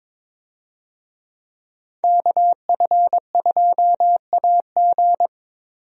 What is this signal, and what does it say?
Morse code, heard as a keyed tone, about 21 wpm, 700 Hz. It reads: KF2AG
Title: USA Call Signs - 5 Characters - Repeated 3x - 22wpm